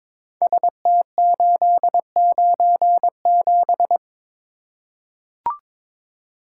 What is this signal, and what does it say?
Morse code, heard as a keyed tone, about 22 wpm, 700 Hz. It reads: ST897
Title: Code Groups - 5 Letter:Numbers - 22wpm